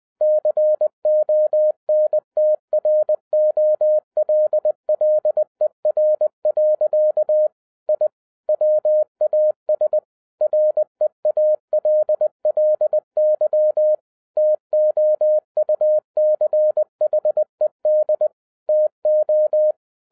Morse code, heard as a keyed tone, about 20 wpm, 610 Hz. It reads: CONTROLLER. I WAS REALLY TOUCHED TO
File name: 2012-01-mp3 / n16-THROUGH EIGHT MONTHS-20wpm